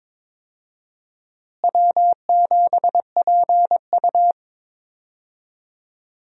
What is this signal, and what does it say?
Morse code, heard as a keyed tone, about 22 wpm, 700 Hz. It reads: W7PU
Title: USA Call Signs - 4 Characters - Repeated 2x - 22wpm